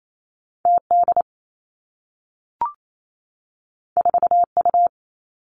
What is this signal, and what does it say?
Morse code, heard as a keyed tone, about 28 wpm, 700 Hz. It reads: T D 4 U
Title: Two Challenging Letter-Number Combinations 28wpm